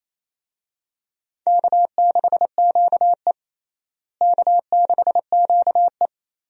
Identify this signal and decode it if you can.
Morse code, heard as a keyed tone, about 28 wpm, 700 Hz. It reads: K6QE K6QE
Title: USA Call Signs - 4 Characters - Repeated 3x - 28wpm